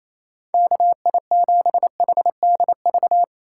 Morse code, heard as a keyed tone, about 28 wpm, 700 Hz. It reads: KI7HDV
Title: USA Call Signs - 6 Characters - 6x Character Spacing - 28wpm